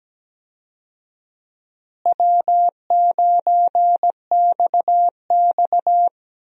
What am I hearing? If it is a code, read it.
Morse code, heard as a keyed tone, about 17 wpm, 700 Hz. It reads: W9XX
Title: USA Call Signs - 4 Characters - Repeated 3x - 17wpm